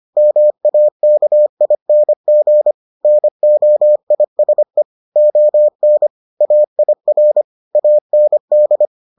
Morse code, heard as a keyed tone, about 25 wpm, 600 Hz. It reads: MAKING NOISE ON AIR AND